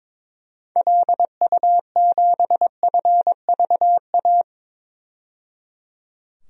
Morse code, heard as a keyed tone, about 22 wpm, 700 Hz. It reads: LU7FVA
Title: DXCC-100 Call Signs - 6 Characters - 22wpm